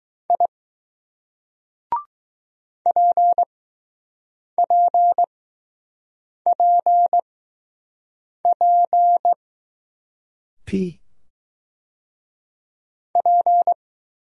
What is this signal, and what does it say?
Morse code, heard as a keyed tone, about 20 wpm, spaced as for 6 wpm, 700 Hz. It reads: I PPPP P